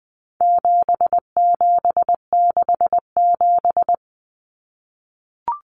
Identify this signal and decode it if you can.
Morse code, heard as a keyed tone, about 20 wpm, 700 Hz. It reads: 7767
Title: Numbers - 4-Digit 20wpm